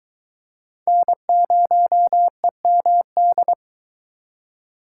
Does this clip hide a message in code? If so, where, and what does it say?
Yes: Morse code from 0.9 to 3.5 s, heard as a keyed tone, about 22 wpm, 700 Hz. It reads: N0EMD